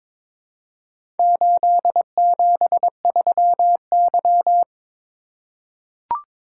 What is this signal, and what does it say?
Morse code, heard as a keyed tone, about 22 wpm, 700 Hz. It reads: 873Y
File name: practice / 22wpm / Four Letter-Number Combinations - 22wpm